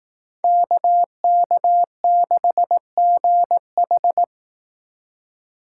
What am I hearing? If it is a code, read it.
Morse code, heard as a keyed tone, about 18 wpm, 700 Hz. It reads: KK6GH
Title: Speed Racing - USA 2x2 Call Signs 15wpm